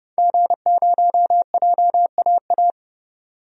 Morse code, heard as a keyed tone, about 30 wpm, 700 Hz. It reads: G0JAA
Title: DXCC-100 Call Signs - 5 Characters - 30wpm